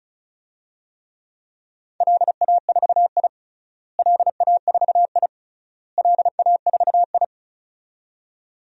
Morse code, heard as a keyed tone, about 34 wpm, 700 Hz. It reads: LA4I LA4I LA4I
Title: DXCC-100 Call Signs - 4 Characters - Repeated 3x - 35wpm